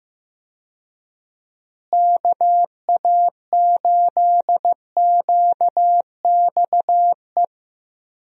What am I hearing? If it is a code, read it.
Morse code, heard as a keyed tone, about 15 wpm, 700 Hz. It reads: KA8QXE
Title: USA Call Signs - 6 Characters - Repeated 3x - 15wpm